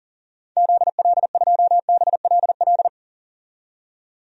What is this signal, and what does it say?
Morse code, heard as a keyed tone, about 40 wpm, 700 Hz. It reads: ZL2BLL
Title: DXCC-100 Call Signs - 6 Characters - 40wpm